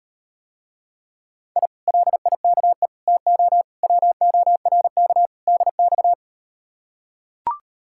Morse code, heard as a keyed tone, about 38 wpm, 700 Hz. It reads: I LIKE TO WORK DX